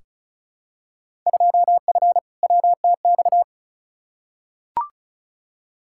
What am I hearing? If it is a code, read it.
Morse code, heard as a keyed tone, about 34 wpm, 700 Hz. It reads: 2F WTX
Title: Field Day Exchanges - 35wpm